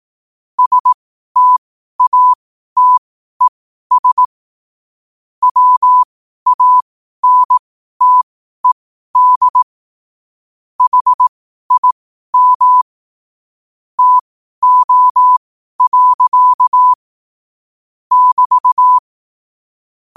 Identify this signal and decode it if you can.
Morse code, heard as a keyed tone, about 18 wpm, spaced as for 12 wpm, 1000 Hz. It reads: STATES WANTED HIM TO. <BT>
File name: qotdfiles / 12wpm